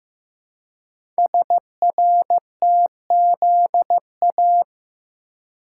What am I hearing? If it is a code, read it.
Morse code, heard as a keyed tone, about 15 wpm, 700 Hz. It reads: SRTZA
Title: Code Groups - 5 Letters - 15wpm